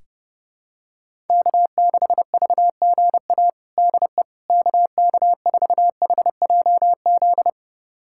Morse code, heard as a keyed tone, about 30 wpm, 700 Hz. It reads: K6VGA DE KK4HJZ
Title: USA Callsign DE Callsign 30wpm